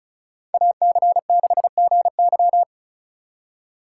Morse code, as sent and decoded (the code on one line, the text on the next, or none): .- -.-. -.... --. -.--
AC6GY